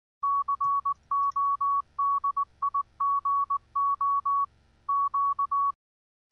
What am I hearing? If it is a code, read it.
Morse code, heard as a keyed tone, about 19 wpm, 1120 Hz. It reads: CODIGO Q